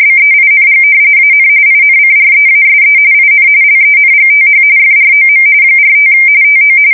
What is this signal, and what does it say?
RTTY, about 45.45 baud. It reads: BEING AN INCREASE IN LENGTH FROM 45 TO
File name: fsk170 / battleships